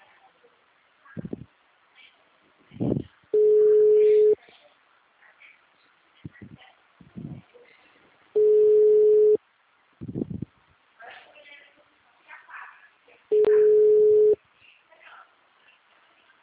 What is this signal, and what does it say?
Ringback tone (Europe, 425 Hz), with speech.
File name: out- -2028- - - .97426